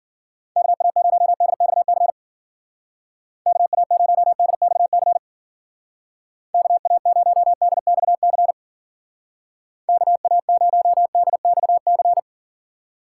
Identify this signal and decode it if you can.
Morse code, heard as a keyed tone, about 51 wpm, 700 Hz. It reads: KA0DXC KA0DXC KA0DXC KA0DXC